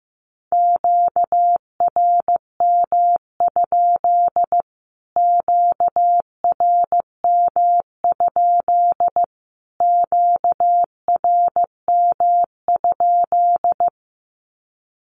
Morse code, heard as a keyed tone, about 15 wpm, 700 Hz. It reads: QRM? QRM? QRM?
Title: Q-Codes Repeated 3x 15wpm